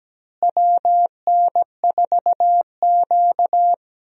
Morse code, heard as a keyed tone, about 17 wpm, 700 Hz. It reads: WN4Q
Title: USA Call Signs - 4 Characters - Repeated 3x - 17wpm